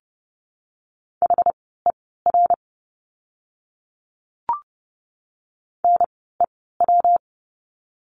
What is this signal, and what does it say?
Morse code, heard as a keyed tone, about 30 wpm, spaced as for 17 wpm, 700 Hz. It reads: HER NEW
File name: course / 30:10wpm / 115 Lesson 19f - Letter w - Words 30wpm